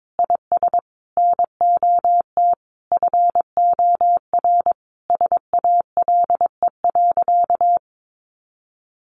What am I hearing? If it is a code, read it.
Morse code, heard as a keyed tone, about 22 wpm, 700 Hz. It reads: IS NOT FOR SALE.
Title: Sentences from Top 1400 Words 22wpm